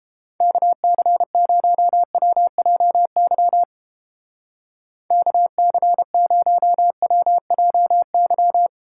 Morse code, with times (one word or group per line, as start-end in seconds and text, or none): 0.4-3.6 KC0WJY
5.1-8.7 KC0WJY